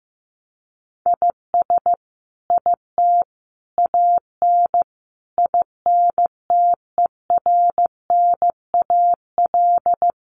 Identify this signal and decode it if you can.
Morse code, heard as a keyed tone, about 15 wpm, 700 Hz. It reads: IS IT AN INTERNAL
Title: Sentences from Top 1400 Words 15wpm